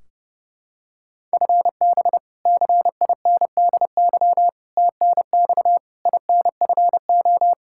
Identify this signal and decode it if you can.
Morse code, heard as a keyed tone, about 32 wpm, 700 Hz. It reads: FB CINDY TNX INFO
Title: Practice QSO - Exchange 3 - More Info - 30wpm